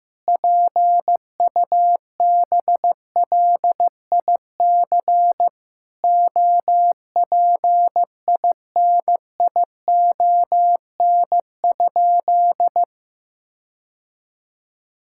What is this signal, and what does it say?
Morse code, heard as a keyed tone, about 16 wpm, 700 Hz. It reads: PUBLIC OPINION?
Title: Sentences from Top 1600 Words 15wpm